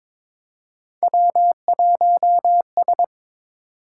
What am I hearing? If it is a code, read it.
Morse code, heard as a keyed tone, about 22 wpm, 700 Hz. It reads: W1S
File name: practice / 22wpm / Three Letter-Number Combinations - 22wpm